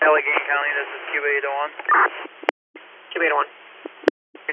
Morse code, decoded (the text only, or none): none